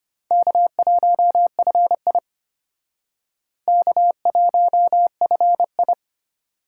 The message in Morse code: K1FI K1FI

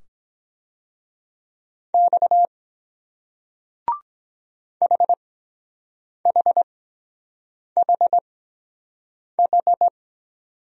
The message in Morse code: X HHHH

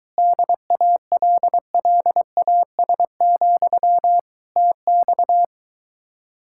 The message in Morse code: DALLAS, TX

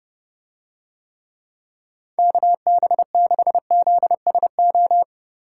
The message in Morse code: KB6ZSO

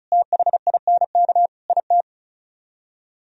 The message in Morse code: THINK IT

